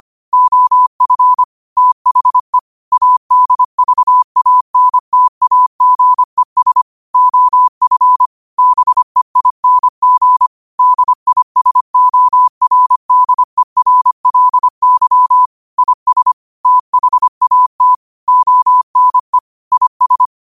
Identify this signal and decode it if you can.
Morse code, heard as a keyed tone, about 25 wpm, 1000 Hz. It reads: OF THE ADVANTAGES OF BEING DISORDERLY IS THAT ONE IS
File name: qotdfiles / 25wpm